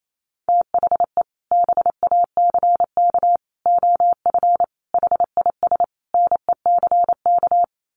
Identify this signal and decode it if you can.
Morse code, heard as a keyed tone, about 28 wpm, 700 Hz. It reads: THE BACK OF HIS NECK
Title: Sets of 5 Words Encore 28wpm